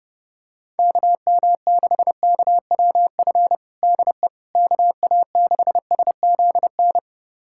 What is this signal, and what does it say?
Morse code, heard as a keyed tone, about 32 wpm, 700 Hz. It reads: KM6KWF DE KA6SZN
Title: USA Callsign DE Callsign - 6 Characters - 30wpm